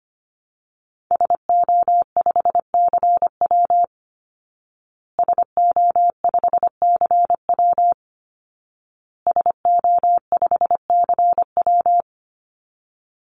Morse code, heard as a keyed tone, about 25 wpm, 700 Hz. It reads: SO5CW SO5CW SO5CW
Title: Call Signs Repeated 2x - 25wpm